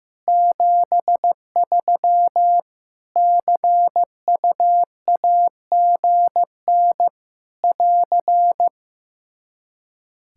Morse code, heard as a keyed tone, about 15 wpm, 700 Hz. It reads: 73 CUAGN <AR>
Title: Practice QSO - Exchange 4 - Ending - 15wpm